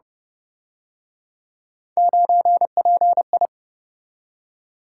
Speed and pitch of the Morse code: 30 wpm, 700 Hz

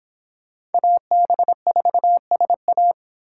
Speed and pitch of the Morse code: 30 wpm, 700 Hz

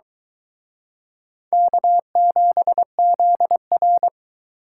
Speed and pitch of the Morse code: 23 wpm, 700 Hz

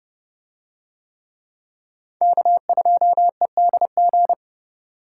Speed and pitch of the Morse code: 29 wpm, 700 Hz

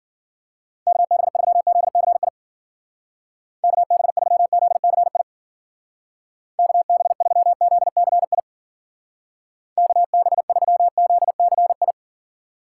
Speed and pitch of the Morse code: 50 wpm, 700 Hz

